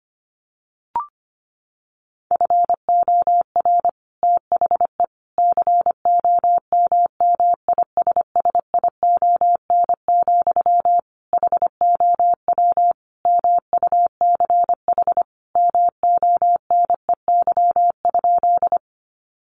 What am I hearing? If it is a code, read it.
Morse code, heard as a keyed tone, about 25 wpm, 700 Hz. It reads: FOR THE COMMISSION, HOW MUCH MONEY?
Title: Sentences from Top 1400 Words 25wpm